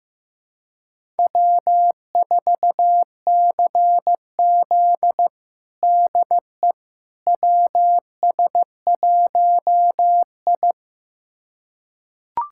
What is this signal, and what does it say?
Morse code, heard as a keyed tone, about 15 wpm, 700 Hz. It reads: W4CZ DE WS1I
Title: USA Callsign DE Callsign - 4 Characters - 15wpm